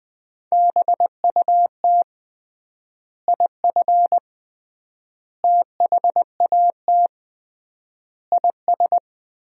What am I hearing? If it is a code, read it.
Morse code, heard as a keyed tone, about 19 wpm, 700 Hz. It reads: BUT IF THAT IS
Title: 3x Word-Spacing - Sentences from Top 200 Words - 20wpm